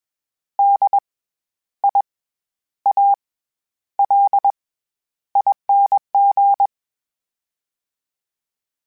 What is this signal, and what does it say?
Morse code, heard as a keyed tone, about 21 wpm, 800 Hz. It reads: D I A L ING
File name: Special INGs-mp3s / Special INGs-mp3s / SlowToFaster ing @20